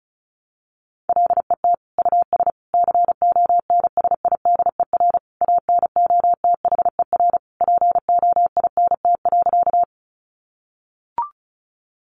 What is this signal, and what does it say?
Morse code, heard as a keyed tone, about 34 wpm, 700 Hz. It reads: LET US CONSIDER ANOTHER POINT.
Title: Sentences from Top 600 Words 35wpm